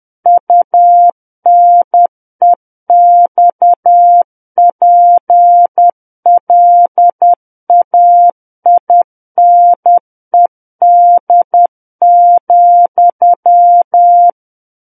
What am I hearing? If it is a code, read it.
Morse code, heard as a keyed tone, about 10 wpm, 700 Hz. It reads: UNEXPLAINED,
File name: Book One / 01-Chapter 1 - 10 WPM